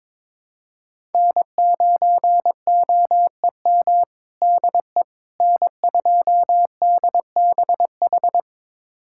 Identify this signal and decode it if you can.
Morse code, heard as a keyed tone, about 22 wpm, 700 Hz. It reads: N9OEM DE N2DBH
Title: USA Callsign DE Callsign - 5 Characters - 22wpm